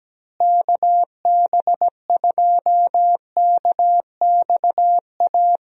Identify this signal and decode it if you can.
Morse code, heard as a keyed tone, about 17 wpm, 700 Hz. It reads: KB2KXA
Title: USA Call Signs - 6 Characters - Repeated 3x - 17wpm